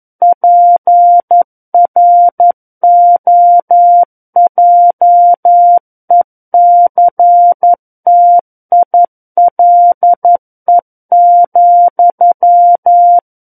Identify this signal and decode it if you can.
Morse code, heard as a keyed tone, about 11 wpm, 700 Hz. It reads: PROJECTILE,
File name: Book One / 02-Chapter 2 - 11 WPM